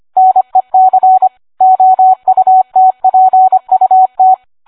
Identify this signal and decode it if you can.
Morse code, heard as a keyed tone, about 25 wpm, 750 Hz. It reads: NEC OUTPUT